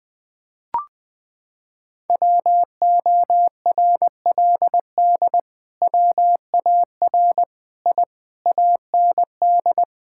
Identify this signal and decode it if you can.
Morse code, heard as a keyed tone, about 20 wpm, 700 Hz. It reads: WORLD WAR I AND
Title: Sentences with 7-Word Phrases - 20wpm